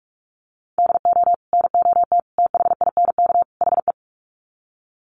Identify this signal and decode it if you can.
Morse code, heard as a keyed tone, about 44 wpm, 700 Hz. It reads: DO NOT THINK HE
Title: Sets of 4 Words 45wpm